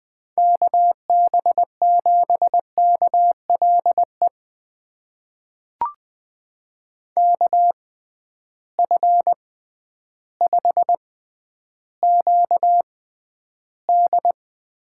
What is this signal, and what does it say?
Morse code, heard as a keyed tone, about 20 wpm, spaced as for 11 wpm, 700 Hz. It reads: KB7KLE KF5QD
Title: USA Call Signs - 6 Characters - 6x Character Spacing - 20wpm